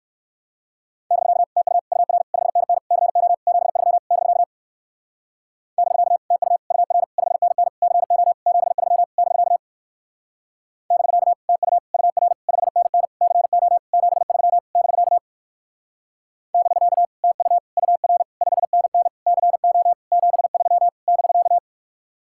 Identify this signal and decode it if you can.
Morse code, heard as a keyed tone, about 60 wpm, 700 Hz. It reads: <BK> TU UR 5NN CO 73 <BK> <BK> TU UR 5NN CO 73 <BK> <BK> TU UR 5NN CO 73 <BK> <BK> TU UR 5NN CO 73 <BK>